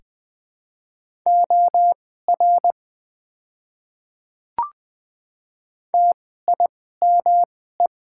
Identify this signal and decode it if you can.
Morse code, heard as a keyed tone, about 20 wpm, spaced as for 14 wpm, 700 Hz. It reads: OR TIME
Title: 2x Character Spacing - Top 100 Words - 20wpm